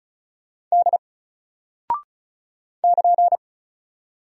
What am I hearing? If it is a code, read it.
Morse code, heard as a keyed tone, about 35 wpm, 700 Hz. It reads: D <KN>
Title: Single Character (Letter:Number:Punctuation:Prosign) 35wpm